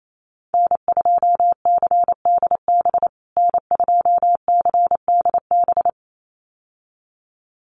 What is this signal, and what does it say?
Morse code, heard as a keyed tone, about 28 wpm, 700 Hz. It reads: N2CDB N2CDB
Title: USA Call Signs Repeated 2x 28wpm